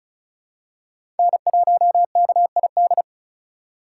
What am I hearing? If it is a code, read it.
Morse code, heard as a keyed tone, about 36 wpm, 700 Hz. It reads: N1KID
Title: USA Call Signs - 8x Character Spacing - 35wpm